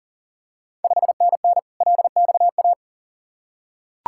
Morse code, heard as a keyed tone, about 40 wpm, 690 Hz. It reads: 5NN LXA
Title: Ontario QSO Party Exchange 40wpm